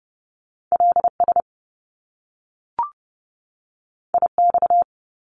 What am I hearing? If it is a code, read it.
Morse code, heard as a keyed tone, about 30 wpm, 700 Hz. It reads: LS IX
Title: Top Two Letter Combinations in Words 30wpm